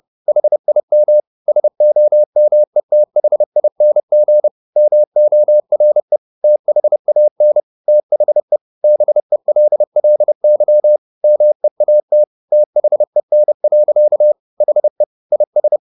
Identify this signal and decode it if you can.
Morse code, heard as a keyed tone, about 30 wpm, 600 Hz. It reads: HIM SOMETHING MORE THAN THE BELLY MEAT THEN. HE IS
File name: the old man and the sea / the old man and the sea 1 30wpm